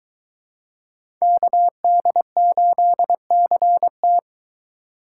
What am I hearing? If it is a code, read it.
Morse code, heard as a keyed tone, about 22 wpm, 700 Hz. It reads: KD8CT